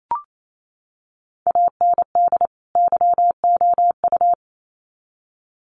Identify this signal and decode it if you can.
Morse code, heard as a keyed tone, about 28 wpm, 700 Hz. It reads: AND YOU
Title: Sets of 2 Words 28wpm